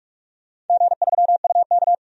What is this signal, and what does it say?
Morse code, heard as a keyed tone, about 46 wpm, 700 Hz. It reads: G3UK
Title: DXCC-100 Call Signs - 4 Characters - 45wpm